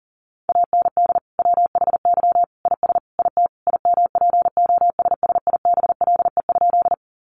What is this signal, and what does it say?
Morse code, heard as a keyed tone, about 38 wpm, 700 Hz. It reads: AND WHY IS IT IMPOSSIBLE?
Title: Sentences from Top 1100 Words 40wpm